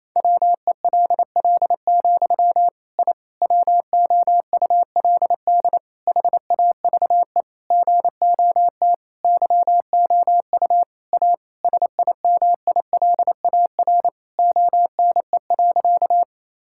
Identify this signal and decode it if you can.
Morse code, heard as a keyed tone, about 28 wpm, 700 Hz. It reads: WELL, I WOULD HAVE GOT YOU A SIMILAR ONE.
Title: Sentences from Top 1000 Words - Review Part 5 - 28wpm